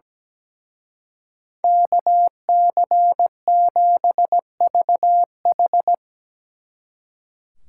Morse code, heard as a keyed tone, about 19 wpm, 700 Hz. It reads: KC7VH
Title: Speed Racing - USA 2x2 Call Signs 17wpm